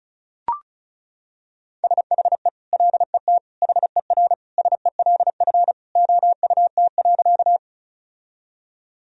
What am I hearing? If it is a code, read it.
Morse code, heard as a keyed tone, about 36 wpm, 700 Hz. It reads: SHE LET HERSELF OUT.
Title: Sentences from Top 1000 Words - Review Part 1 - 35wpm